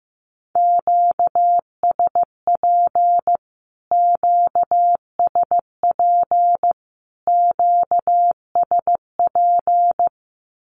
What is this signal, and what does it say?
Morse code, heard as a keyed tone, about 15 wpm, 700 Hz. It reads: QSP QSP QSP
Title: Q-Codes Repeated 3x 15wpm